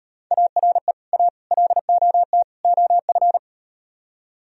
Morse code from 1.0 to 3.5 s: A LOT OF

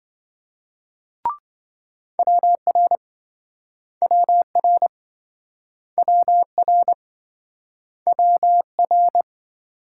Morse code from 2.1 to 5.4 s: WR WR

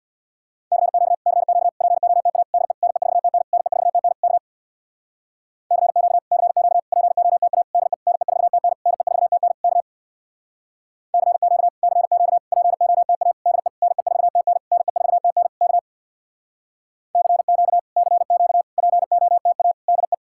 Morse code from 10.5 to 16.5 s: CQ CQ POTA DE NE4TN NE4TN K

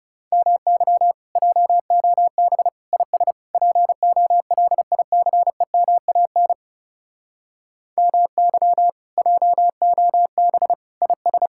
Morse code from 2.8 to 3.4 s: IS

